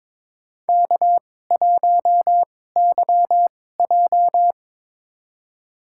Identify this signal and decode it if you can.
Morse code, heard as a keyed tone, about 22 wpm, spaced as for 16 wpm, 700 Hz. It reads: K1YJ
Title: USA Call Signs - 4 Characters - 2x Character Spacing - 22wpm